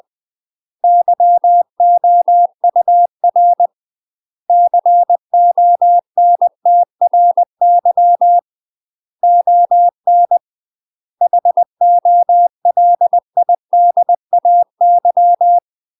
Morse code, heard as a keyed tone, about 20 wpm, 700 Hz. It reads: YOUR CONTRY ON HOLIDAY
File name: UK MASTER QSOs TRAINING / 20WPM / UK MASTER QSOs 0060